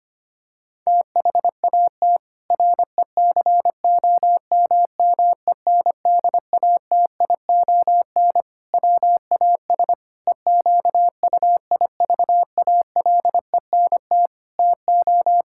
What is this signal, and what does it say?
Morse code, heard as a keyed tone, about 25 wpm, 700 Hz. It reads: THAT RECOMMENDATION WAS EQUIVALENT TO
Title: Sentences from Top 1600 Words 25wpm